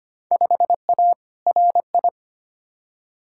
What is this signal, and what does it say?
Morse code, heard as a keyed tone, about 25 wpm, 700 Hz. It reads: 5A RI
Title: Field Day Exchanges Encore - 25wpm